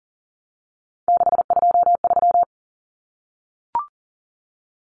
Morse code, heard as a keyed tone, about 40 wpm, 700 Hz. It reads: 623
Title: Common Cut Numbers - 3-Digit 40wpm